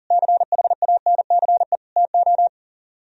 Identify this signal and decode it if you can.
Morse code, heard as a keyed tone, about 50 wpm, 700 Hz. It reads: CHANCE TO